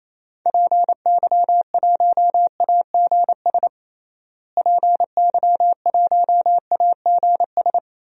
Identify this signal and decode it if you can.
Morse code, heard as a keyed tone, about 28 wpm, 700 Hz. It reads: PY1AGS PY1AGS
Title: DXCC-100 Call Signs - 6 Characters - Repeated 2x - 28wpm